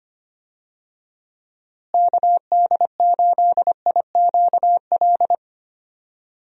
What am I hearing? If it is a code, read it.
Morse code, heard as a keyed tone, about 25 wpm, 700 Hz. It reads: KD8IQL
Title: USA Call Signs - 6 Characters - Repeated 3x - 25wpm